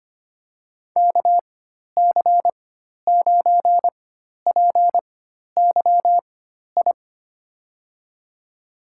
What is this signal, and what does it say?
Morse code, heard as a keyed tone, about 25 wpm, spaced as for 12 wpm, 700 Hz. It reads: KC9PYI
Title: USA Call Signs - 4x Character Spacing - 25wpm